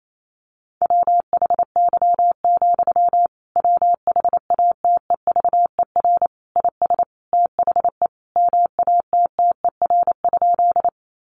Morse code, heard as a keyed tone, about 28 wpm, 700 Hz. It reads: WHY, WHATEVER IS THE MATTER?
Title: Sentences from Top 1100 Words 28wpm